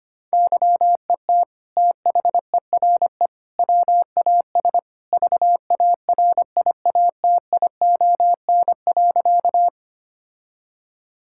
Morse code, heard as a keyed tone, about 25 wpm, 700 Hz. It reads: YET THERE WAS VARIATION.
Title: Sentences from Top 1600 Words 25wpm